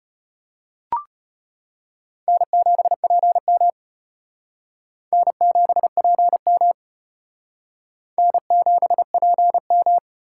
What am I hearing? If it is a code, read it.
Morse code, heard as a keyed tone, about 34 wpm, 700 Hz. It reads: N7PM N7PM N7PM